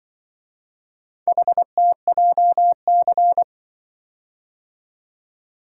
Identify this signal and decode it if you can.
Morse code, heard as a keyed tone, about 24 wpm, 700 Hz. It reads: HTJC
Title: Speed-Racing - Four Letter-Number Combinations - 20wpm